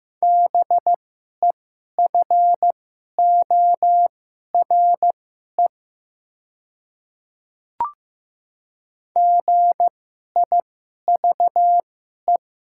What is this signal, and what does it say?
Morse code, heard as a keyed tone, about 15 wpm, spaced as for 11 wpm, 700 Hz. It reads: BEFORE GIVE
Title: 2x Character Spacing - Top 200 Words - 15wpm